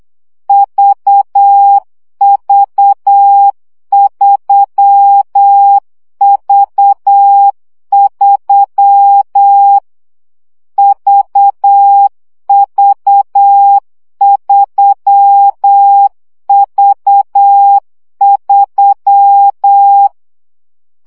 Morse code, heard as a keyed tone, about 8 wpm, 800 Hz. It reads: VV3V3 VV3V3
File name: CW gyakorlas v 3 7 baud